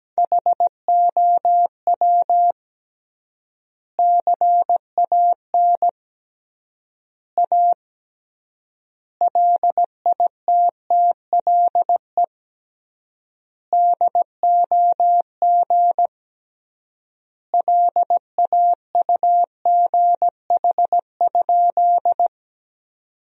Morse code, heard as a keyed tone, about 17 wpm, 700 Hz. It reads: HOW CAN A LITTLE DOG LAUGH?
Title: 3x Word-Spacing - Sentences from Top 500 Words - 17wpm